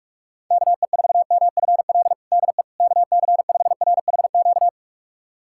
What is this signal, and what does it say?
Morse code, heard as a keyed tone, about 41 wpm, 700 Hz. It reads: KE4MFL DE KC5RHQ
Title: USA Callsign DE Callsign 45wpm